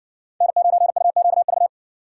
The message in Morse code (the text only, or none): N9UQV